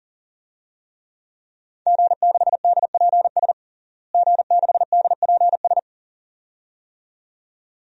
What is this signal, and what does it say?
Morse code, heard as a keyed tone, about 40 wpm, 700 Hz. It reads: G6DPS G6DPS
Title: DXCC-100 Call Signs - 5 Characters - Repeated 2x - 40wpm